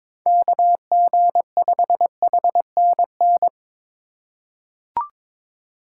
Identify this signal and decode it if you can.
Morse code, heard as a keyed tone, about 23 wpm, 700 Hz. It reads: KG5HNN
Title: USA Call Signs - 6 Characters - 3x Character Spacing - 22wpm